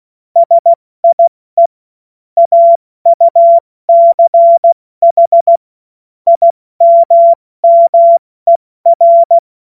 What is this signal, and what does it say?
Morse code, heard as a keyed tone, about 16 wpm, 680 Hz. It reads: SIE AUCH IMMER
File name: Morsekurs / Witze / Witz 17.Oktober 2023 70 F80